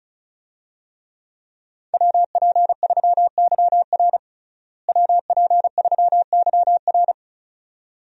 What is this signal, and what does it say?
Morse code, heard as a keyed tone, about 36 wpm, 700 Hz. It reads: WP3YR WP3YR
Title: USA Call Signs - 5 Characters - Repeated 2x - 35wpm